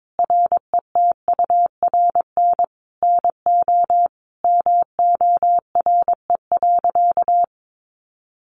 Morse code, heard as a keyed tone, about 22 wpm, 700 Hz. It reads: RETURN NO MORE.
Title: Sentences from Top 800 Words 22wpm